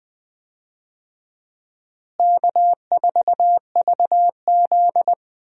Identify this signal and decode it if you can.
Morse code, heard as a keyed tone, about 20 wpm, 700 Hz. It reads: K4VZ
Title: USA Call Signs - 4 Characters - Repeated 3x - 20wpm